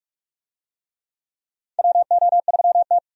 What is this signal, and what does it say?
Morse code, heard as a keyed tone, about 46 wpm, 700 Hz. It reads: WO3T
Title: USA Call Signs - 4 Characters - 45wpm